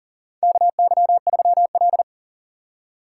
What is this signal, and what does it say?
Morse code, heard as a keyed tone, about 40 wpm, 700 Hz. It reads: KY3L
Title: USA Call Signs - 4 Characters - 3x Character Spacing - 40wpm